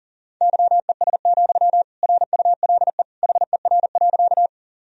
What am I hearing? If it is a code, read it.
Morse code, heard as a keyed tone, about 40 wpm, 700 Hz. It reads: YES, RULE HER.
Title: Sentences from Top 1000 Words - Review Part 2 - 40wpm